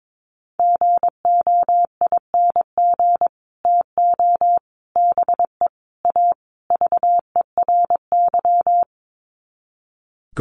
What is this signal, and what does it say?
Morse code, heard as a keyed tone, about 22 wpm, 700 Hz. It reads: GOING TO BE A VERY
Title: Sets of 5 Words Encore 22wpm